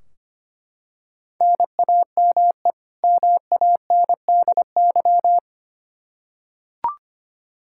Morse code, heard as a keyed tone, about 25 wpm, 700 Hz. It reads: NAME MANDY
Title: Practice QSO - Exchange 1 - Beginning - 25wpm